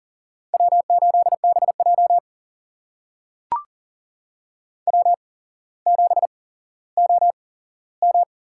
Morse code, heard as a keyed tone, about 39 wpm, spaced as for 17 wpm, 700 Hz. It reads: W8BJ W7OM
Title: USA Call Signs - 4 Characters - 8x Character Spacing - 40wpm